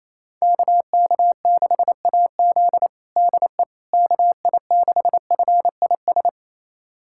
Morse code, heard as a keyed tone, about 27 wpm, 700 Hz. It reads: KK6AZ DE KI6FIS
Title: USA Callsign DE Callsign 28wpm